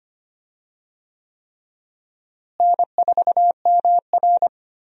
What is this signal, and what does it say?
Morse code, heard as a keyed tone, about 25 wpm, 700 Hz. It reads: N4MR
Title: USA Call Signs - 4 Characters - Repeated 3x - 25wpm